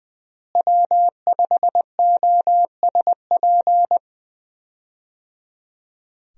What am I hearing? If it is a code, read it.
Morse code, heard as a keyed tone, about 20 wpm, 700 Hz. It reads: W5OSP
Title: USA Call Signs - 5 Characters - Repeated 3x - 20wpm